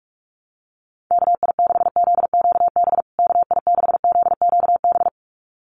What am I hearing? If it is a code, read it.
Morse code, heard as a keyed tone, about 44 wpm, 700 Hz. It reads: KI6ZQB KI6ZQB
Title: USA Call Signs Repeated 2x 45wpm